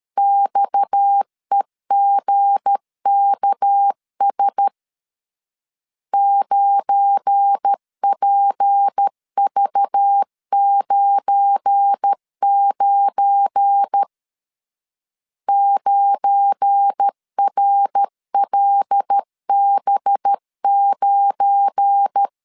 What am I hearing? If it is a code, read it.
Morse code, heard as a keyed tone, about 13 wpm, 800 Hz. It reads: XEGKS 9PV99 9RLB9
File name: LEZIONI / 7 GF9Z / 7 GF9 12wpm 2319